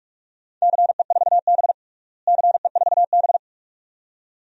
Morse code, heard as a keyed tone, about 41 wpm, 700 Hz. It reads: CE4B CE4B
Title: DXCC-100 Call Signs - 4 Characters - Repeated 2x - 45wpm